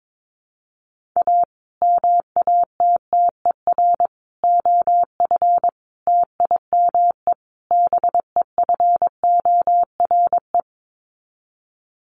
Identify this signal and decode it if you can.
Morse code, heard as a keyed tone, about 22 wpm, 700 Hz. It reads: A MATTER OF TIME BEFORE
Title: Sets of 5 Words Encore 22wpm